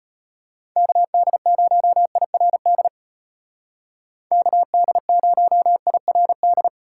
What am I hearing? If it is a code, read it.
Morse code, heard as a keyed tone, about 31 wpm, 700 Hz. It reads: KD0IRD KD0IRD
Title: Speed Racing - USA 2x3 Call Signs 25wpm